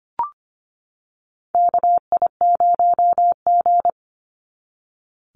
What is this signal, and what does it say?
Morse code, heard as a keyed tone, about 25 wpm, 700 Hz. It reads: KI0G
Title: NA SOTA Chaser Callsigns 25wpm